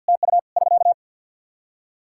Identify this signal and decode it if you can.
Morse code, heard as a keyed tone, about 50 wpm, 700 Hz. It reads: TU <SK>
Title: K1USN SST QSO Practice 50wpm